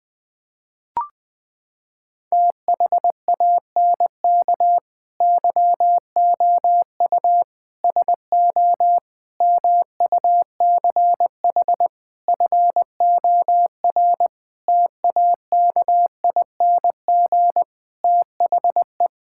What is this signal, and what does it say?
Morse code, heard as a keyed tone, about 20 wpm, 700 Hz. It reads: THANK YOU SO MUCH FOR TAKING THE
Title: Sets of 9 Words - 20wpm